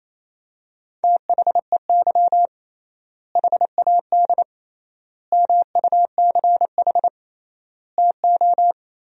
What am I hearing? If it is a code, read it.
Morse code, heard as a keyed tone, about 28 wpm, 700 Hz. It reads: THEY HAD MUCH TO
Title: 3x Word-Spacing - Sentences from Top 400 Words - 28wpm